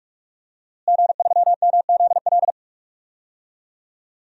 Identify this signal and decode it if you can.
Morse code, heard as a keyed tone, about 41 wpm, 700 Hz. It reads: G3MZL
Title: DXCC-100 Call Signs - 5 Characters - Repeated 3x - 45wpm